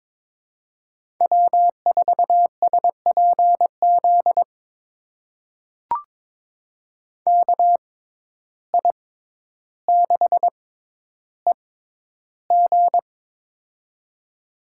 Morse code, heard as a keyed tone, about 22 wpm, spaced as for 11 wpm, 700 Hz. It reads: W4SPZ KI6EG
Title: USA Call Signs - 5 Characters - 6x Character Spacing - 22wpm